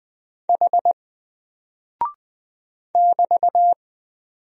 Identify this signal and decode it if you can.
Morse code, heard as a keyed tone, about 20 wpm, 700 Hz. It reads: H <BT>
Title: Single Character (Letter:Number:Punctuation:Prosign) Alt 20wpm